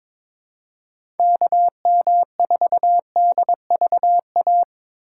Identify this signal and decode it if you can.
Morse code, heard as a keyed tone, about 22 wpm, 700 Hz. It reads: KM4DVA
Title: USA Call Signs - 6 Characters - 3x Character Spacing - 22wpm